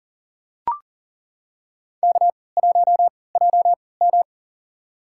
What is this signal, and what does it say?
Morse code, heard as a keyed tone, about 40 wpm, spaced as for 23 wpm, 700 Hz. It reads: K1JM
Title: USA Call Signs - 4 Characters - 3x Character Spacing - 40wpm